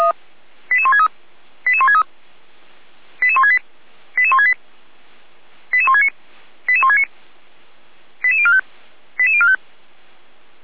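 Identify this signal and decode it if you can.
Five-tone selective call (ZVEI), heard as quick runs of five tones, tones of about 70 ms. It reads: 88163 88163 88168 88168 88169 88169 88045 88045
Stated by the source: ZVEI five-tone selective call 88064 received via SDR.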